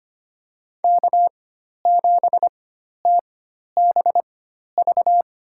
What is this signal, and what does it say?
Morse code, heard as a keyed tone, about 25 wpm, spaced as for 12 wpm, 700 Hz. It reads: K7TBV
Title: USA Call Signs - 5 Characters - 4x Character Spacing - 25wpm